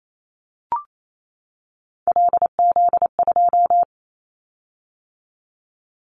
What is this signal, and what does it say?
Morse code, heard as a keyed tone, about 28 wpm, 700 Hz. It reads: LZ2
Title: Callsign Prefixes for Top 100 DXCC - 28wpm